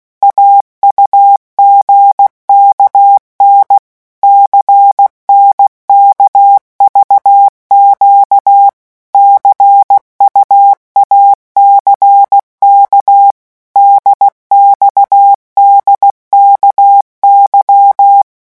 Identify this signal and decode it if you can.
Morse code, heard as a keyed tone, about 16 wpm, 800 Hz. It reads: AUGKN CNKVQ CUACK DXDKY
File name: morse mp3 / revision3-vitesse 15 mots